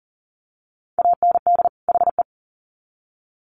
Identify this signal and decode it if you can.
Morse code, heard as a keyed tone, about 40 wpm, 700 Hz. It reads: AND HE
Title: Sets of 2 Words 40wpm